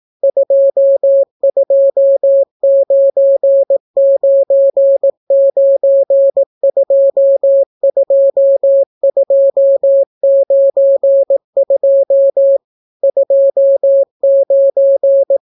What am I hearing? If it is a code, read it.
Morse code, heard as a keyed tone, about 18 wpm, 550 Hz. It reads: 2299922292 29